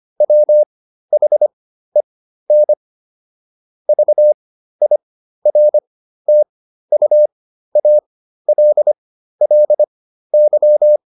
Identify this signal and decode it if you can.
Morse code, heard as a keyed tone, about 25 wpm, spaced as for 13 wpm, 600 Hz. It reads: WHEN VIRTUALLY